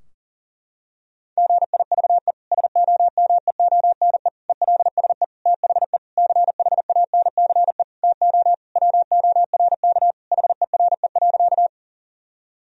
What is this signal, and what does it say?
Morse code, heard as a keyed tone, about 40 wpm, 700 Hz. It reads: GIVE SOMEONE ELSE THE CHANCE TO WORK HERE.
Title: Sentences from Top 1000 Words - Review Part 4 - 40wpm